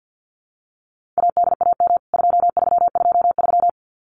Morse code, heard as a keyed tone, about 50 wpm, 700 Hz. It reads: ADAM 2323
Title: CWT Exchange 50wpm